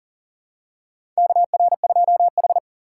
Morse code, heard as a keyed tone, about 40 wpm, 700 Hz. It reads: KR2H
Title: USA Call Signs - 4 Characters - 6x Character Spacing - 40wpm